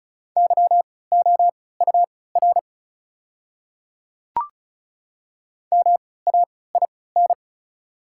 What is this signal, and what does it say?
Morse code, heard as a keyed tone, about 36 wpm, spaced as for 20 wpm, 700 Hz. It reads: YOUR MAIN
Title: 3x Character Spacing - Top 300 Words - 35wpm